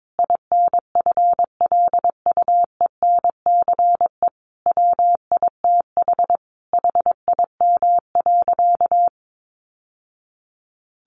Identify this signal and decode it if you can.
Morse code, heard as a keyed tone, about 22 wpm, 700 Hz. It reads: INFLUENCE WITH HIM.
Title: Sentences from Top 1100 Words 22wpm